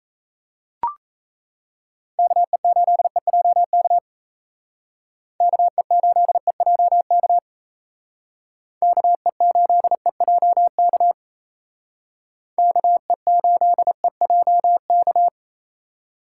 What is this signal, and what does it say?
Morse code, heard as a keyed tone, about 36 wpm, 700 Hz. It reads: KE8EJK KE8EJK KE8EJK KE8EJK